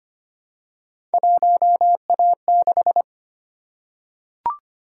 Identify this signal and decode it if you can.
Morse code, heard as a keyed tone, about 25 wpm, 700 Hz. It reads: 1A6
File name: Three Letter-Number Combinations - 25wpm